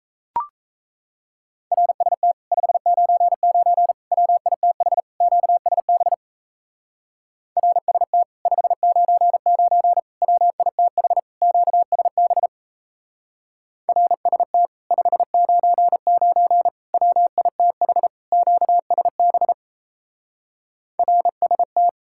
Morse code, heard as a keyed tone, about 34 wpm, 700 Hz. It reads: RST 599 WITH QSB RST 599 WITH QSB RST 599 WITH QSB RST
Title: Speed-Racing Common QSO Phrases Encore - 28wpm